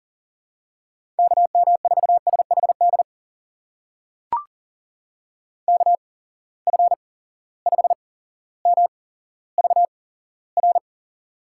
Morse code, heard as a keyed tone, about 40 wpm, spaced as for 17 wpm, 700 Hz. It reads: KM4SHD KF5MVR
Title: USA Call Signs - 6 Characters - 8x Character Spacing - 40wpm